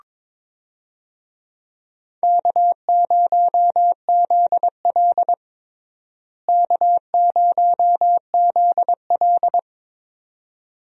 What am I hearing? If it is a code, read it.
Morse code, heard as a keyed tone, about 21 wpm, 700 Hz. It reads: K0ZL K0ZL
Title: USA Call Signs - 4 Characters - Repeated 2x - 22wpm